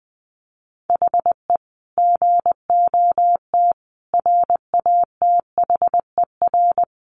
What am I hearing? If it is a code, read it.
Morse code, heard as a keyed tone, about 20 wpm, 700 Hz. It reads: HE GOT RATHER
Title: Sentences from Top 700 Words 20wpm